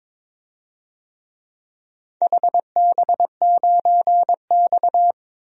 Morse code, heard as a keyed tone, about 21 wpm, 700 Hz. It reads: HB9X
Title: DXCC-100 Call Signs - 4 Characters - 22wpm